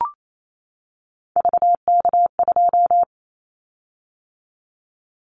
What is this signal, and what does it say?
Morse code, heard as a keyed tone, about 28 wpm, 700 Hz. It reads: VK2
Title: Callsign Prefixes for Top 100 DXCC - 28wpm